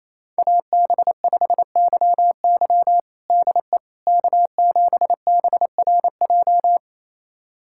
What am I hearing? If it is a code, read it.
Morse code, heard as a keyed tone, about 28 wpm, 700 Hz. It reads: AB5YY DE K7BRJ
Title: USA Callsign DE Callsign - 5 Characters - 28wpm